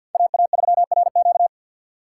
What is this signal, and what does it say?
Morse code, heard as a keyed tone, about 48 wpm, 700 Hz. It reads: AA3RQ
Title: USA Call Signs - 5 Characters - 2x Character Spacing - 50wpm